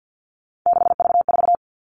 Morse code, heard as a keyed tone, about 50 wpm, 700 Hz. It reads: 6V4
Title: All Cut Numbers - 3-Digit 50wpm